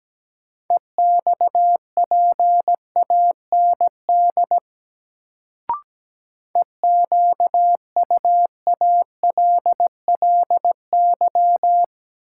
Morse code, heard as a keyed tone, about 17 wpm, 700 Hz. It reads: EXPAND EQUALLY
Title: Sentences from Top 1400 Words - New Word Drill - 17wpm